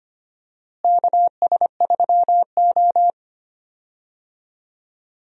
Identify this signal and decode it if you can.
Morse code, heard as a keyed tone, about 25 wpm, 700 Hz. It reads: KS3O
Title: USA Call Signs - 4 Characters - Repeated 3x - 25wpm